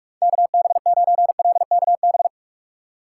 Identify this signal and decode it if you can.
Morse code, heard as a keyed tone, about 44 wpm, 700 Hz. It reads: KB9LKB
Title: USA Call Signs - 6 Characters - 3x Character Spacing - 45wpm